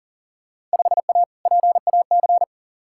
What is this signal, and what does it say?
Morse code, heard as a keyed tone, about 37 wpm, 700 Hz. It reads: 5A PAC
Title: Field Day Exchanges Encore - 40wpm